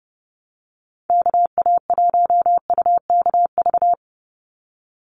Morse code, heard as a keyed tone, about 30 wpm, 700 Hz. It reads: KA1UKV
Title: USA Call Signs 30wpm